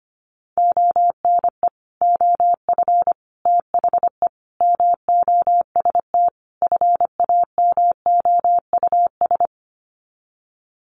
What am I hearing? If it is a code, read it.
Morse code, heard as a keyed tone, about 25 wpm, 700 Hz. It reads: ONE OF THE MOST FAMOUS
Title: Sets of 5 Words Encore2 25wpm